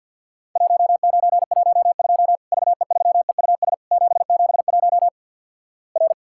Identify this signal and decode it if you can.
Morse code, heard as a keyed tone, about 51 wpm, 700 Hz. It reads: 1912 VE3EUS 771 R
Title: Worked All Europe DX Contest - QTC Exchange - 50wpm